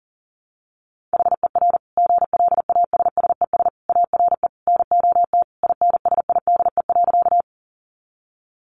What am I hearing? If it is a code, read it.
Morse code, heard as a keyed tone, about 37 wpm, 700 Hz. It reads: HER GLASSES ARE NOT INSIDE.
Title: Additional Easy Phrases 40wpm